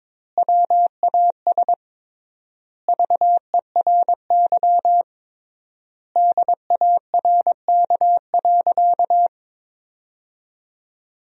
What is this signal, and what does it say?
Morse code, heard as a keyed tone, about 22 wpm, 700 Hz. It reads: WAS VERY DARK.
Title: 3x Word-Spacing - Sentences from Top 500 Words - 22wpm